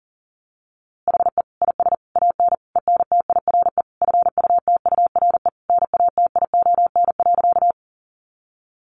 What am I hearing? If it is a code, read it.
Morse code, heard as a keyed tone, about 40 wpm, 700 Hz. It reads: HE IS AN ENTIRE FUTURE NATION.
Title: A Sound Of Thunder 40wpm